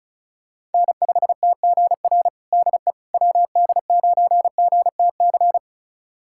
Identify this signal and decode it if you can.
Morse code, heard as a keyed tone, about 36 wpm, 700 Hz. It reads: N5TGR DE WD9GTC
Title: USA Callsign DE Callsign 35wpm